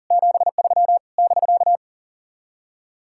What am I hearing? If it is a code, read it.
Morse code, heard as a keyed tone, about 54 wpm, spaced as for 47 wpm, 700 Hz. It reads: 73 <BK>
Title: Speed-Racing Common QSO Phrases Encore - 40wpm